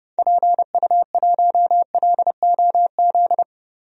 Morse code, heard as a keyed tone, about 30 wpm, 700 Hz. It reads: PU1LOZ
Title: DXCC-100 Call Signs - 6 Characters - 30wpm